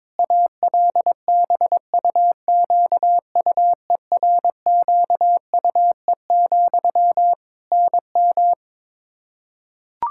Morse code, heard as a keyed tone, about 22 wpm, 700 Hz. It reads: ALBUQUERQUE, NM
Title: Practice QSO - Exchange 1 - Beginning - 22wpm